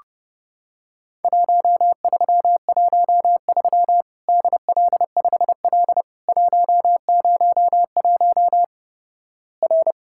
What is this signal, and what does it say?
Morse code, heard as a keyed tone, about 32 wpm, 700 Hz. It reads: 1313 DL5L 101 R
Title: Worked All Europe DX Contest - QTC Exchange - 30wpm